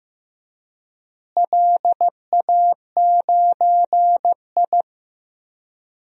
Morse code, heard as a keyed tone, about 15 wpm, 700 Hz. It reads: LA9I
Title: DXCC-100 Call Signs - 4 Characters - Repeated 3x - 15wpm